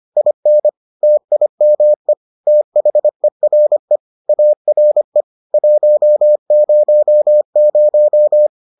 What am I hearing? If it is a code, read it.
Morse code, heard as a keyed tone, about 25 wpm, 600 Hz. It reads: IN TIME THERE ARE 100